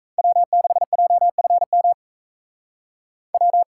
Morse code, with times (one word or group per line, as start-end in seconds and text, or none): 0.2-1.9 W6JFM
3.3-3.6 W